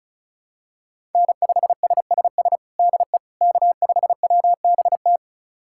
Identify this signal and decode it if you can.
Morse code, heard as a keyed tone, about 34 wpm, 700 Hz. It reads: N5SSS DE K5WBT
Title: USA Callsign DE Callsign - 5 Characters - 35wpm